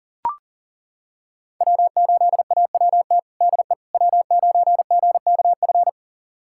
Morse code, heard as a keyed tone, about 37 wpm, 700 Hz. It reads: W8AWT DE W9GKF
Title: USA Callsign DE Callsign - 5 Characters - 40wpm